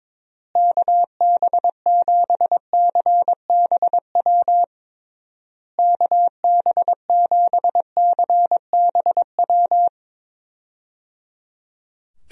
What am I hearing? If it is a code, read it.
Morse code, heard as a keyed tone, about 23 wpm, 700 Hz. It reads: KB7CBW KB7CBW
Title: USA Call Signs - 6 Characters - Repeated 3x - 22wpm